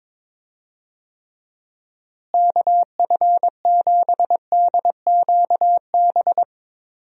Morse code, heard as a keyed tone, about 22 wpm, 700 Hz. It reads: KF7DQB
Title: USA Call Signs - 6 Characters - Repeated 2x - 22wpm